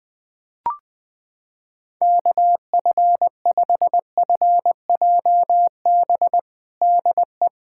Morse code, heard as a keyed tone, about 21 wpm, 700 Hz. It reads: KF5FJB DE
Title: USA Callsign DE Callsign - 6 Characters - 20wpm